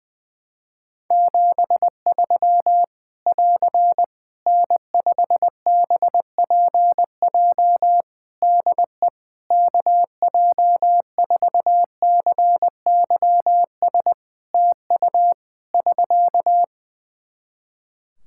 Morse code, heard as a keyed tone, about 21 wpm, 700 Hz. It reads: 73 <AR> N5BPJ DE KJ4CYS TU <SK>
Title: Practice QSO - Exchange 4 - Ending - 20wpm